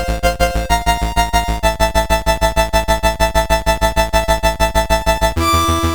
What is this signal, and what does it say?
There is no code.